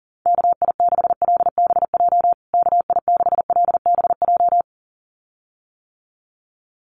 Morse code, heard as a keyed tone, about 40 wpm, 700 Hz. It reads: KI6LBJ KI6LBJ
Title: USA Call Signs Repeated 2x 40wpm